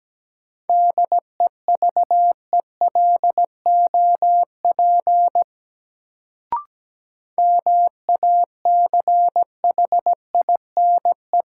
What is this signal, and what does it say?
Morse code, heard as a keyed tone, about 17 wpm, 700 Hz. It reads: DEVELOP MACHINE
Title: Sentences from Top 1000 Words - Review Part 2 - New Word Drill - 17wpm